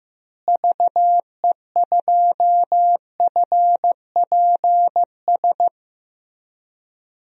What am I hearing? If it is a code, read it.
Morse code, heard as a keyed tone, about 15 wpm, 700 Hz. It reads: VE2FPS
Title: DXCC-100 Call Signs - 6 Characters - Repeated 3x - 15wpm